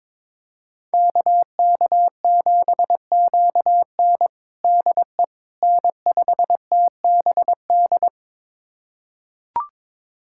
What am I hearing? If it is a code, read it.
Morse code, heard as a keyed tone, about 22 wpm, 700 Hz. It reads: KK7QN DE N5TBD
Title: USA Callsign DE Callsign - 5 Characters - 22wpm